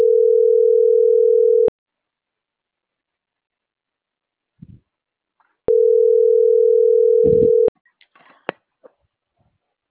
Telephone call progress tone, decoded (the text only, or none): ringback tone